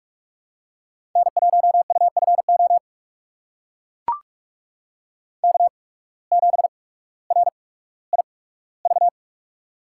Morse code, heard as a keyed tone, about 42 wpm, spaced as for 19 wpm, 700 Hz. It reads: N1UFO K7RIV
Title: USA Call Signs - 8x Character Spacing - 45wpm